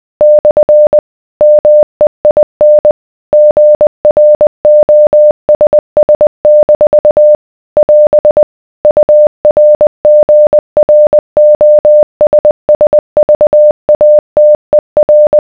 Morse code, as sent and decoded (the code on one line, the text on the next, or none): -..-. / -- . .. -. / --. .-. --- ... ... -....- / .-... / ..- .-. --. .-. --- ... ... ...- .- - . .-.
/ MEIN GROSS- <AS> URGROSSVATER